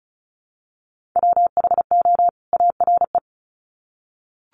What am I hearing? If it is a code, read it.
Morse code, heard as a keyed tone, about 34 wpm, 700 Hz. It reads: WHO ARE
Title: Sets of 2 Words 35wpm